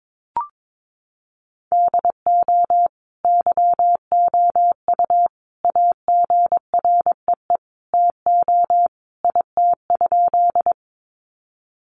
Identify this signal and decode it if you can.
Morse code, heard as a keyed tone, about 22 wpm, 700 Hz. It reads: DO YOU AGREE TO IT?
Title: Sentences from Top 1100 Words 22wpm